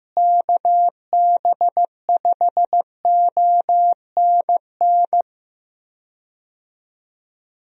Morse code, heard as a keyed tone, about 15 wpm, 700 Hz. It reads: KB5ONN
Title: USA Call Signs - 6 Characters - Repeated 3x - 15wpm